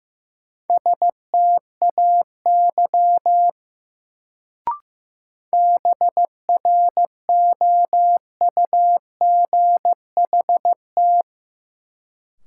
Morse code, heard as a keyed tone, about 15 wpm, 700 Hz. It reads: STAY BROUGHT
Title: Sentences from Top 500 Words - New Word Drill - 15wpm